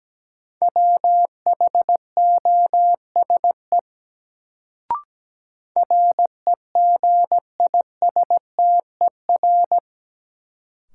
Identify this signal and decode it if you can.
Morse code, heard as a keyed tone, about 17 wpm, 700 Hz. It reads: WHOSE REGISTER
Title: Sentences from Top 1200 Words - New Word Drill - 17wpm